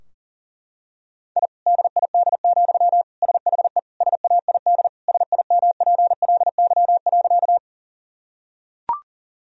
Morse code, heard as a keyed tone, about 40 wpm, 700 Hz. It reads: I DID, SHE SAID SIMPLY.
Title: Sentences from Top 1000 Words - Review Part 4 - 40wpm